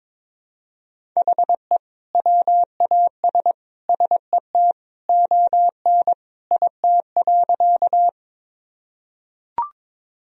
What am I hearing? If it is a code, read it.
Morse code, heard as a keyed tone, about 22 wpm, 700 Hz. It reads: HE WAS SET ON IT.
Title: Sentences from Top 1000 Words - Review Part 1 - 22wpm